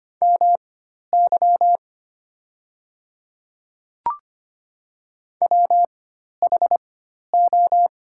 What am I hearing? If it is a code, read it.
Morse code, heard as a keyed tone, about 25 wpm, spaced as for 12 wpm, 700 Hz. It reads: MY WHO
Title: 4x Character Spacing - Top 100 Words - 25wpm